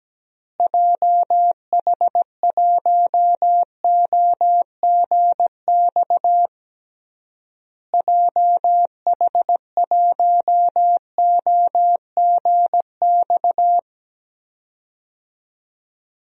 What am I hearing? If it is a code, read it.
Morse code, heard as a keyed tone, about 17 wpm, 700 Hz. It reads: JH1OGX JH1OGX
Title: DXCC-100 Call Signs - 6 Characters - Repeated 3x - 17wpm